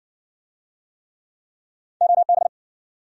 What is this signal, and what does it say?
Morse code, heard as a keyed tone, about 55 wpm, 700 Hz. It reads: OB